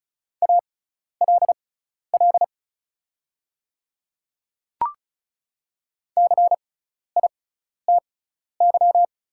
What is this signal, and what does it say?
Morse code, heard as a keyed tone, about 36 wpm, spaced as for 12 wpm, 690 Hz. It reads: ALL CITY
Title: 6x Character Spacing - Top 500 Words - 35wpm